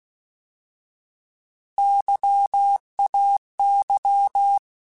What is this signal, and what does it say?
Morse code, heard as a keyed tone, about 16 wpm, 800 Hz. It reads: YAY
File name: Alphabetical / Alphabetical mp3s / Ys @15